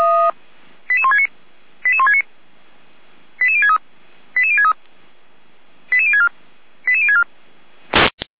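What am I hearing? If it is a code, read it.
Five-tone selective call (ZVEI), heard as quick runs of five tones, tones of about 70 ms. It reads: 88169 88169 88063 88063 88064 88064
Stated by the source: ZVEI five-tone selective call received on an SDR.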